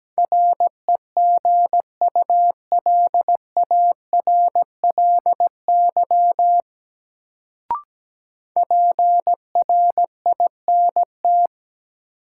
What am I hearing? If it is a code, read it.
Morse code, heard as a keyed tone, about 17 wpm, 700 Hz. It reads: REGULARLY PRINT
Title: Sentences from Top 1400 Words - New Word Drill Alt - 17wpm